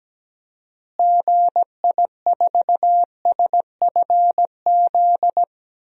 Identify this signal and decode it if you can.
Morse code, heard as a keyed tone, about 17 wpm, 700 Hz. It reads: GI4SFZ
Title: DXCC-100 Call Signs - 6 Characters - Repeated 2x - 17wpm